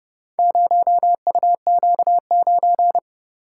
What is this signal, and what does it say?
Morse code, heard as a keyed tone, about 31 wpm, 700 Hz. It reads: 0UQ9